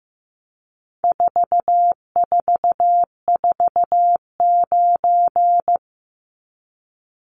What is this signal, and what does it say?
Morse code, heard as a keyed tone, about 15 wpm, 700 Hz. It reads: 4449
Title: Numbers - 4-Digit 15wpm